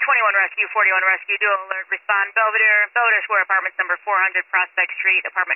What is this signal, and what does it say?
Emergency Alert System (SAME) tones, heard as harsh buzzing bursts, with speech. It is heard from 0.7 to 4.7 s.